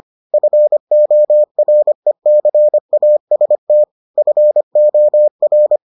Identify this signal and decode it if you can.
Morse code, heard as a keyed tone, about 25 wpm, 600 Hz. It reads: FORECAST FOR